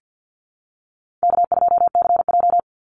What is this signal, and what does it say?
Morse code, heard as a keyed tone, about 50 wpm, 700 Hz. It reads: K2CJ
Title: Easy Call Signs Repeated 2x - 50wpm